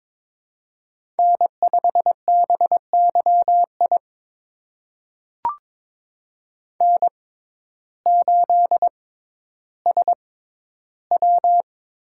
Morse code, heard as a keyed tone, about 22 wpm, spaced as for 12 wpm, 700 Hz. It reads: N5BYI N8SW
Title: USA Call Signs - 5 Characters - 6x Character Spacing - 22wpm